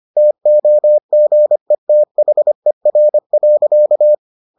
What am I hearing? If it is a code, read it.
Morse code, heard as a keyed tone, about 25 wpm, 600 Hz. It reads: TOGETHER.